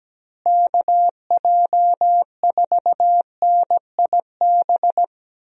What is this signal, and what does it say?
Morse code, heard as a keyed tone, about 17 wpm, 700 Hz. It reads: KJ4NIB
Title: USA Call Signs - 6 Characters - Repeated 3x - 17wpm